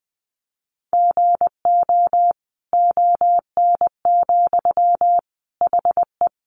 Morse code, heard as a keyed tone, about 20 wpm, 700 Hz. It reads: GO ON, HE
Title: Sentences from Top 100 Words 20wpm